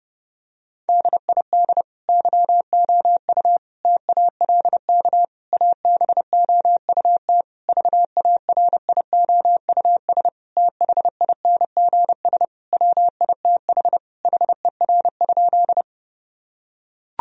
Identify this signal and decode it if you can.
Morse code, heard as a keyed tone, about 31 wpm, 700 Hz. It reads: DID YOU TALK ABOUT VARIOUS THINGS WITH HER?
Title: Sentences from Top 1000 Words - Review Part 4 - 30wpm